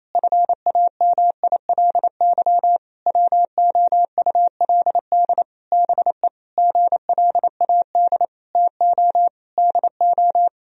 Morse code, heard as a keyed tone, about 28 wpm, 700 Hz. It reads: FAMILY WOULD BE GLAD TO DO
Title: Sentences from Top 1000 Words - Review Part 3 - 28wpm